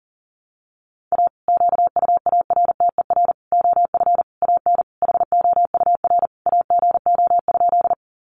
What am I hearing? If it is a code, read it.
Morse code, heard as a keyed tone, about 40 wpm, 700 Hz. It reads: A QUARTER OF AN HOUR AGO?
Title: Sentences from Top 1300 Words 40wpm